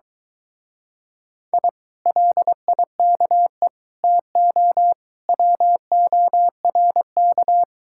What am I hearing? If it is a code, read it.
Morse code, heard as a keyed tone, about 23 wpm, 700 Hz. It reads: I LIKE TO WORK